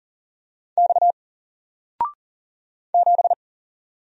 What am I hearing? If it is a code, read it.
Morse code, heard as a keyed tone, about 40 wpm, 700 Hz. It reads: X 7
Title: Single Letter-Number 40wpm